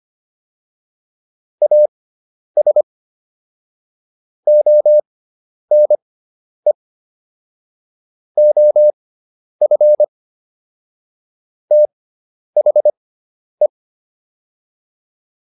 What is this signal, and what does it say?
Morse code, heard as a keyed tone, about 25 wpm, spaced as for 10 wpm, 600 Hz. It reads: AS ONE OF THE